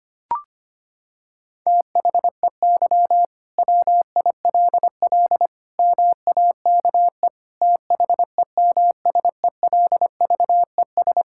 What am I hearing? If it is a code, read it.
Morse code, heard as a keyed tone, about 25 wpm, 700 Hz. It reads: THEY WILL MAKE THEMSELVES
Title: Sentences from Top 1600 Words 25wpm